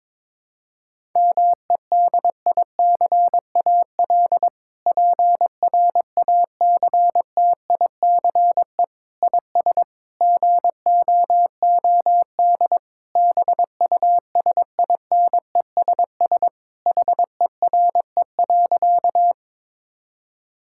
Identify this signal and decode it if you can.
Morse code, heard as a keyed tone, about 22 wpm, 700 Hz. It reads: MEDICAL PRACTICE IS GOOD BUSINESS HERE.
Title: Sentences from Top 1000 Words - Review Part 5 - 22wpm